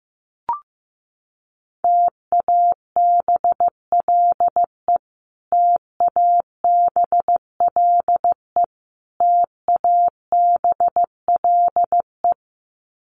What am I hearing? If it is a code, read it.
Morse code, heard as a keyed tone, about 15 wpm, 700 Hz. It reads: TABLE TABLE TABLE
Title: Top 500 Words Repeated 15wpm